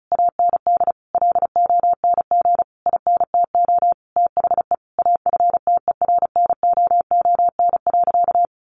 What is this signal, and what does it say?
Morse code, heard as a keyed tone, about 34 wpm, 700 Hz. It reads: AND LONG INTO THE AFTERNOON.
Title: Sentences from Top 1500 Words 35wpm